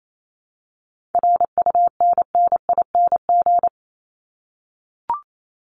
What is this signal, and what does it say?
Morse code, heard as a keyed tone, about 28 wpm, 700 Hz. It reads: RUNNING
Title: ING Suffix - Top 100 Words with Base Words 28wpm